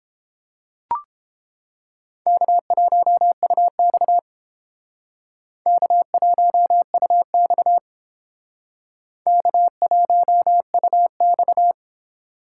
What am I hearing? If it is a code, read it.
Morse code, heard as a keyed tone, about 30 wpm, 700 Hz. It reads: K1UX K1UX K1UX